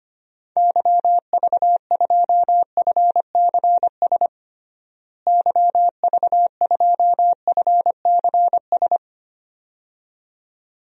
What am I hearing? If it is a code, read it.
Morse code, heard as a keyed tone, about 25 wpm, 700 Hz. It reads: YV2FCS YV2FCS
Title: DXCC-100 Call Signs - 6 Characters - Repeated 2x - 25wpm